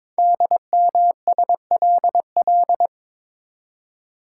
Morse code, heard as a keyed tone, about 22 wpm, 700 Hz. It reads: DMSLL
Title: Code Groups - 5 Letter:Numbers - 22wpm